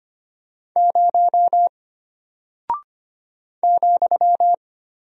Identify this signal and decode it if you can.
Morse code, heard as a keyed tone, about 25 wpm, 700 Hz. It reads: 0 ,
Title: Single Character (Letter:Number:Punctuation:Prosign) Alt 25wpm